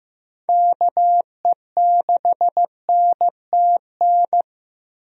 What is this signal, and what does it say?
Morse code, heard as a keyed tone, about 15 wpm, 700 Hz. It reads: KE6NTN
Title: USA Call Signs - 6 Characters - Repeated 3x - 15wpm